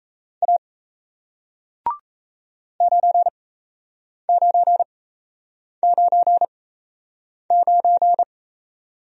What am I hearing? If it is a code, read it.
Morse code, heard as a keyed tone, about 38 wpm, spaced as for 8 wpm, 700 Hz. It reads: A 9999